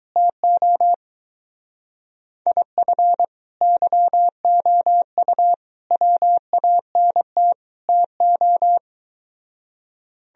Morse code, heard as a keyed tone, about 22 wpm, 700 Hz. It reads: TO IF YOU WANT TO